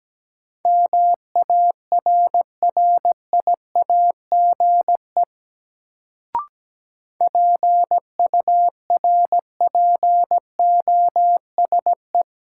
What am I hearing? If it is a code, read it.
Morse code, heard as a keyed tone, about 17 wpm, 700 Hz. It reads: MARRIAGE PURPOSE
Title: Sentences from Top 800 Words - New Word Drill - 17wpm